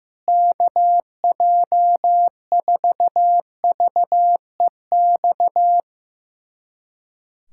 Morse code, heard as a keyed tone, about 15 wpm, 700 Hz. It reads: KJ4VEX
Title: USA Callsign DE Callsign - 6 Characters - 15wpm